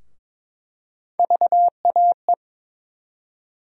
Morse code, heard as a keyed tone, about 21 wpm, 700 Hz. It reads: VAE
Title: USA Call Sign Postfixes - 3 Letters 22wpm